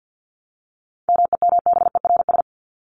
Morse code, heard as a keyed tone, about 50 wpm, 700 Hz. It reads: MEMBERS
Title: ES:S Suffix - Top 100 Words with Base Words 50wpm